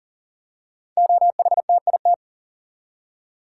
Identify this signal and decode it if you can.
Morse code, heard as a keyed tone, about 41 wpm, 690 Hz. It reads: OHTIT
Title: Code Groups - 5 Letters - 40wpm